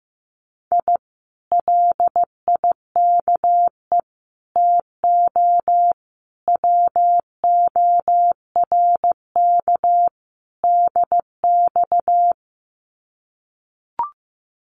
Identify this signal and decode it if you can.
Morse code, heard as a keyed tone, about 15 wpm, 700 Hz. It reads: I LIKE TO WORK DX
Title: Common QSO Phrases Encore 15wpm